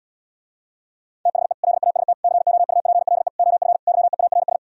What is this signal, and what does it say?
Morse code, heard as a keyed tone, about 79 wpm, 700 Hz. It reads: THE LAST COUPLE OF YEARS